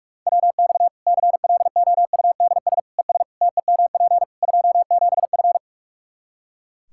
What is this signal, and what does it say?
Morse code, heard as a keyed tone, about 46 wpm, 700 Hz. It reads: WX CLOUDS ES TEMP 27F
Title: Practice QSO - Exchange 2 - Info - 45wpm